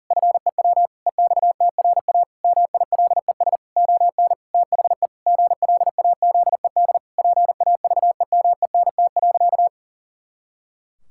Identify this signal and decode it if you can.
Morse code, heard as a keyed tone, about 40 wpm, 700 Hz. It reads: FEW EXTRA MILES ON THE GLAZED PAVEMENT.
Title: Pangram Challenge v2 - 40wpm